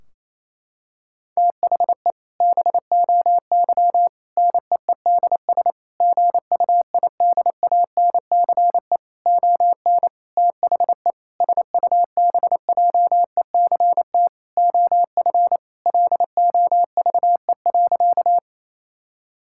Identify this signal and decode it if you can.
Morse code, heard as a keyed tone, about 28 wpm, 700 Hz. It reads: THE BOY NEEDS GUIDANCE ON THE SUBJECT OF LOVE.
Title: Sentences from Top 1600 Words 28wpm